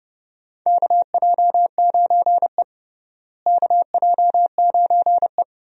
Morse code, heard as a keyed tone, about 30 wpm, 700 Hz. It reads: KJ9E KJ9E
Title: USA Call Signs - 4 Characters - Repeated 3x - 30wpm